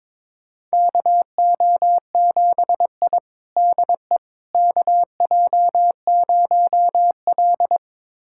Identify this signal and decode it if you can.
Morse code, heard as a keyed tone, about 22 wpm, 700 Hz. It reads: KO7I DE KJ0L
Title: USA Callsign DE Callsign - 4 Characters - 22wpm